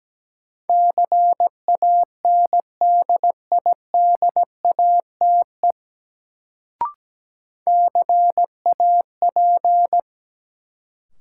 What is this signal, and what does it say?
Morse code, heard as a keyed tone, about 17 wpm, 700 Hz. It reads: CANDIDATE CAP
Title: Sentences from Top 1600 Words - New Word Drill - 17wpm